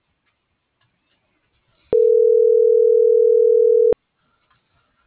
Ringback tone (North America), 440 + 480 Hz.